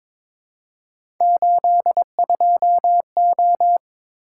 Morse code, heard as a keyed tone, about 22 wpm, 700 Hz. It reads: 82O
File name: practice / 22wpm / Three Letter-Number Combinations - 22wpm